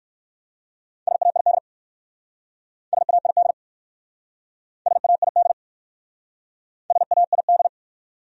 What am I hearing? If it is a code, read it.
Morse code, heard as a keyed tone, about 54 wpm, 700 Hz. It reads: SAID SAID SAID SAID
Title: Speed-Racing Top 200 Words 45wpm